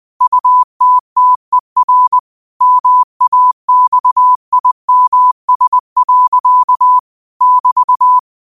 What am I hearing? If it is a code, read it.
Morse code, heard as a keyed tone, about 20 wpm, 1000 Hz. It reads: UTTER MAXIMS. <BT>